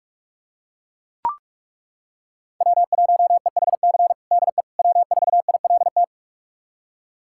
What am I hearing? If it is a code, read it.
Morse code, heard as a keyed tone, about 42 wpm, 700 Hz. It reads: W1EHC DE W4ILT
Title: USA Callsign DE Callsign - 5 Characters - 45wpm